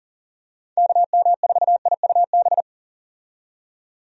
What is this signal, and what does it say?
Morse code, heard as a keyed tone, about 40 wpm, 700 Hz. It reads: KM4IUB
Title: USA Call Signs - 6 Characters - 40wpm